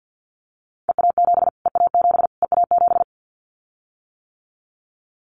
Morse code, heard as a keyed tone, about 50 wpm, 700 Hz. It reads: EA7 EA7 EA7
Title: Callsign Prefixes for Top 25 DXCC - Repeated 3x - 50wpm